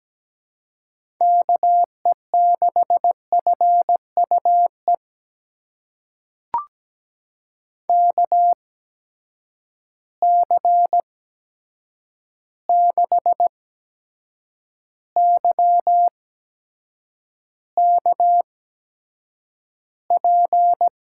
Morse code, heard as a keyed tone, about 17 wpm, spaced as for 7 wpm, 700 Hz. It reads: KE6FUE KC6YKP
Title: USA Call Signs - 6 Characters - 8x Character Spacing - 17wpm